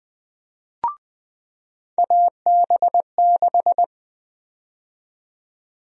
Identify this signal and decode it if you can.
Morse code, heard as a keyed tone, about 20 wpm, 700 Hz. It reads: AB6
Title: USA Call Sign Prefixes - 20wpm